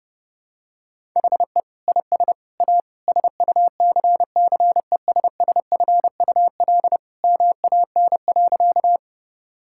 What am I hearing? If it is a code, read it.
Morse code, heard as a keyed tone, about 32 wpm, 700 Hz. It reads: HE IS A SUCCESSFUL MAN.
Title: Sentences from Top 1000 Words - Review Part 6 - 30wpm